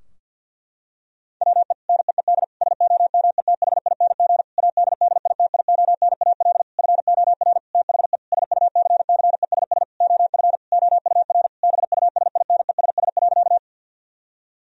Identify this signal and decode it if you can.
Morse code, heard as a keyed tone, about 50 wpm, 700 Hz. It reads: WE NEED SOMETHING ADDITIONAL FOR THE SUCCESS OF OUR BUSINESS.
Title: Sentences from Top 1000 Words - Review Part 4 - 50wpm